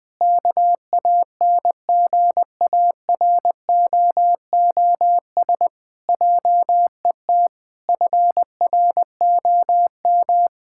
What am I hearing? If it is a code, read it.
Morse code, heard as a keyed tone, about 20 wpm, 700 Hz. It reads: KANGAROOS JET FROM
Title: Pangram Challenge v2 - 20wpm